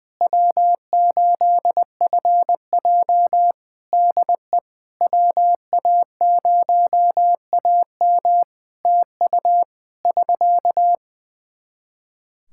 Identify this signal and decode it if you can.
Morse code, heard as a keyed tone, about 20 wpm, 700 Hz. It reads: W8FJ DE WA0AM TU <SK>
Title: Practice QSO - Exchange 4 - Ending - 20wpm